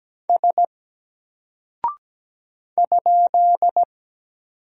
Morse code, heard as a keyed tone, about 17 wpm, 700 Hz. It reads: S ?
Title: Single Character (Letter:Number:Punctuation:Prosign) 17wpm